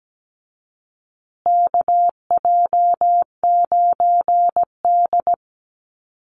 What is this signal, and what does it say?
Morse code, heard as a keyed tone, about 17 wpm, 700 Hz. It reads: KJ9D
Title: USA Call Signs Repeated 3x 17wpm